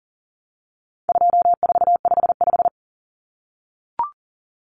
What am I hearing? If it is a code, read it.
Morse code, heard as a keyed tone, about 40 wpm, 700 Hz. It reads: 2455
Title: Common Cut Numbers - 4-Digit 40wpm